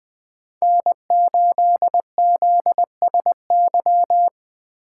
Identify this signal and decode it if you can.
Morse code, heard as a keyed tone, about 20 wpm, 700 Hz. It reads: N8ZSY
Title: Speed Racing - USA 1x3 Call Signs 17wpm